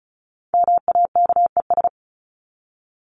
Morse code, heard as a keyed tone, about 34 wpm, 700 Hz. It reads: MAKES
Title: ES:S Suffix - Top 100 Words with Base Words 35wpm